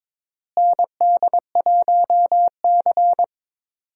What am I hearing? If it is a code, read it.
Morse code, heard as a keyed tone, about 22 wpm, 700 Hz. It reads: ND1C
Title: USA Call Signs - 4 Characters - Repeated 3x - 22wpm